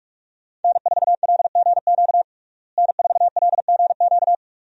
Morse code, heard as a keyed tone, about 46 wpm, 700 Hz. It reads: N4LGQ N4LGQ
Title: USA Call Signs - 5 Characters - Repeated 2x - 45wpm